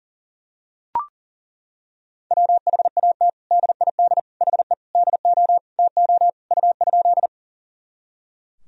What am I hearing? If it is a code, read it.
Morse code, heard as a keyed tone, about 40 wpm, 700 Hz. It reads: WHAT DID HE DO TO U?
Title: Sentences from Top 100 Words and Abbreviations 40wpm